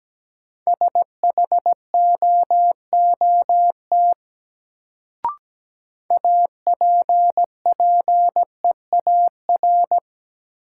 Morse code, heard as a keyed tone, about 17 wpm, 700 Hz. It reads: SHOOT APPEAR
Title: Sentences from Top 1200 Words - New Word Drill Alt - 17wpm